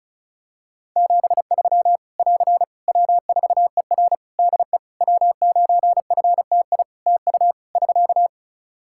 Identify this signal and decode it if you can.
Morse code, heard as a keyed tone, about 36 wpm, 700 Hz. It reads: 73 <AR> W4ER DE W9FTI TU <SK>
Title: Practice QSO - Exchange 4 - Ending - 35wpm